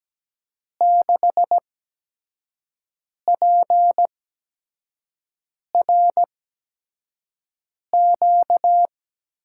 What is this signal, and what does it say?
Morse code, heard as a keyed tone, about 17 wpm, spaced as for 5 wpm, 700 Hz. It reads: 6PRQ
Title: USA Call Signs - 5 Characters - 8x Character Spacing - 17wpm